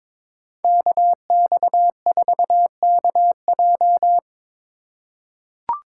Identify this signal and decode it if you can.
Morse code, heard as a keyed tone, about 22 wpm, 700 Hz. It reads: KX4KJ
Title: USA Call Signs - 5 Characters - 4x Character Spacing - 22wpm